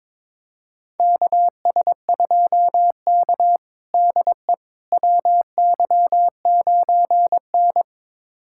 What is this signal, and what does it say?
Morse code, heard as a keyed tone, about 22 wpm, 700 Hz. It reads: KS2K DE WY9N
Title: USA Callsign DE Callsign - 4 Characters - 22wpm